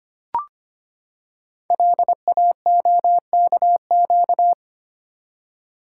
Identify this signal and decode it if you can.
Morse code, heard as a keyed tone, about 25 wpm, 700 Hz. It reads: LAOKQ
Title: Code Groups - 5 Letters - 25wpm